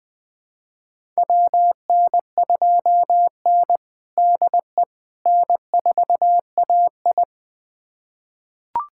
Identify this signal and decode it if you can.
Morse code, heard as a keyed tone, about 20 wpm, 700 Hz. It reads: WN2N DE N4AI
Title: USA Callsign DE Callsign - 4 Characters - 20wpm